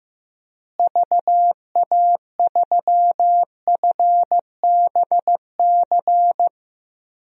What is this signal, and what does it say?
Morse code, heard as a keyed tone, about 15 wpm, 700 Hz. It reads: VA3FBC
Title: DXCC-100 Call Signs - 6 Characters - Repeated 2x - 15wpm